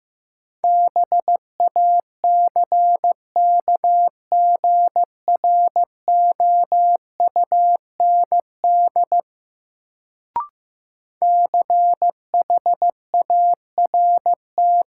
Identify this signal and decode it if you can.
Morse code, heard as a keyed tone, about 15 wpm, 700 Hz. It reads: BACKGROUND CHART
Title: Sentences from Top 1300 Words - New Word Drill Alt - 15wpm